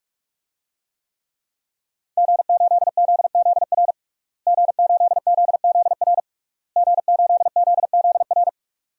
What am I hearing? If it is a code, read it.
Morse code, heard as a keyed tone, about 46 wpm, 700 Hz. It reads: G8ZZR G8ZZR G8ZZR
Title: DXCC-100 Call Signs - 5 Characters - Repeated 3x - 45wpm